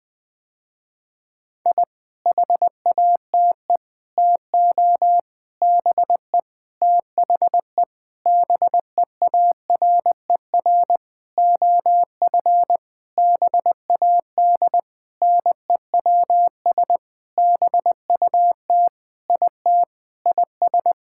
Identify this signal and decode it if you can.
Morse code, heard as a keyed tone, about 21 wpm, 700 Hz. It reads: I HATE TO BE THE BEARER OF BAD NEWS BUT IT IS
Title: Sentences with 9-Word Phrases - Spoken Beginning - 20wpm